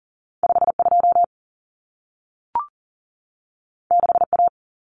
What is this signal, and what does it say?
Morse code, heard as a keyed tone, about 40 wpm, 700 Hz. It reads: 52 6A
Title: Common Cut Numbers - 2-Digit 40wpm